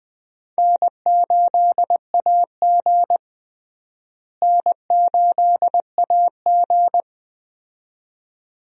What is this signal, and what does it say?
Morse code, heard as a keyed tone, about 20 wpm, 700 Hz. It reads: N8AG N8AG
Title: USA Call Signs - 4 Characters - Repeated 2x - 20wpm